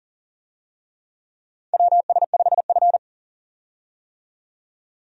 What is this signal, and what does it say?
Morse code, heard as a keyed tone, about 40 wpm, 700 Hz. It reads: WS5F
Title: USA Call Signs - 4 Characters - 40wpm